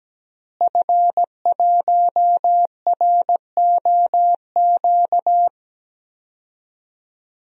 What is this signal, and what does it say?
Morse code, heard as a keyed tone, about 17 wpm, 700 Hz. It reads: F1ROQ
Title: DXCC-100 Call Signs - 5 Characters - Repeated 3x - 17wpm